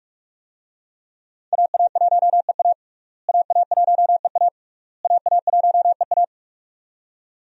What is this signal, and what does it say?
Morse code, heard as a keyed tone, about 46 wpm, 700 Hz. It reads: AA1EA AA1EA AA1EA
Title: USA Call Signs - 5 Characters - Repeated 3x - 45wpm